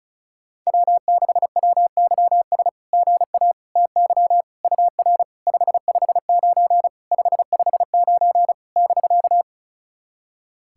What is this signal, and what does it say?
Morse code, heard as a keyed tone, about 36 wpm, 690 Hz. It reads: W6WYS GA TY UR 559 559 <BK>
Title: SOTA QSO Practice 35wpm